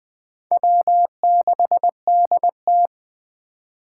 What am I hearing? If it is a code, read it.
Morse code, heard as a keyed tone, about 20 wpm, 700 Hz. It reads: W6DT
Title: 2023 SOTA Chaser Call Signs - 20wpm